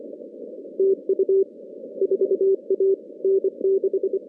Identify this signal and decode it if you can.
Morse code, heard as a keyed tone, about 26 wpm, 410 Hz. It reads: TU 4A N6
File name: contests / CW-SS-k5tr- -2100